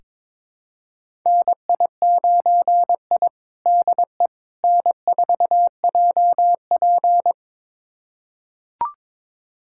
Morse code, heard as a keyed tone, about 22 wpm, 700 Hz. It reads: NI9I DE N4JP
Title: USA Callsign DE Callsign - 4 Characters - 22wpm